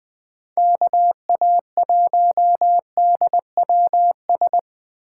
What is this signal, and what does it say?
Morse code, heard as a keyed tone, about 23 wpm, 700 Hz. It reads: KA1DWS
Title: Speed Racing - USA Call Signs 20wpm